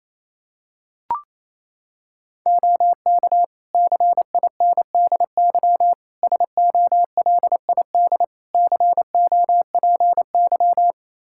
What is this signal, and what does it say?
Morse code, heard as a keyed tone, about 28 wpm, 700 Hz. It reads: OK CINDY SOLID COPY
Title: K1USN SST QSO Practice 28wpm